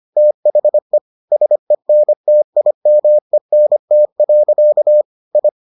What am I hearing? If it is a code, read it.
Morse code, heard as a keyed tone, about 25 wpm, 600 Hz. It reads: THE SENTIMENT. I